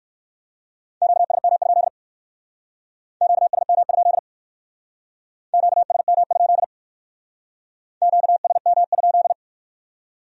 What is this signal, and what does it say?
Morse code, heard as a keyed tone, about 64 wpm, 700 Hz. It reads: QSM? QSM? QSM? QSM?